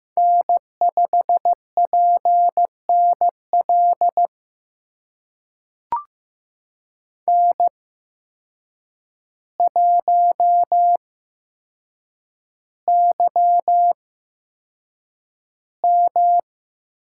Morse code, heard as a keyed tone, about 15 wpm, spaced as for 7 wpm, 700 Hz. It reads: N5PNL N1YM
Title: USA Call Signs - 5 Characters - 8x Character Spacing - 15wpm